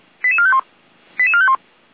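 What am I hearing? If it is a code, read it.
Five-tone selective call (ZVEI), heard as quick runs of five tones, tones of about 70 ms. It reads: 88451 88451
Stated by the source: ZVEI five-tone call received by SDR, Feuerwehr Orlamünde, channel 455.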